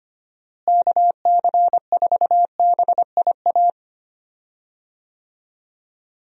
Morse code, heard as a keyed tone, about 25 wpm, 700 Hz. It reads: KC4BIA
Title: USA Call Signs - 6 Characters - Repeated 2x - 25wpm